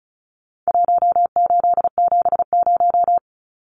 Morse code, heard as a keyed tone, about 34 wpm, 700 Hz. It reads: 1870
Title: Numbers - Contest Style - 4-Digit 35wpm